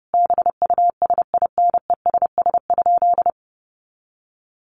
Morse code, heard as a keyed tone, about 30 wpm, 700 Hz. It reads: BUSINESS?
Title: Sentences from Top 1300 Words 30wpm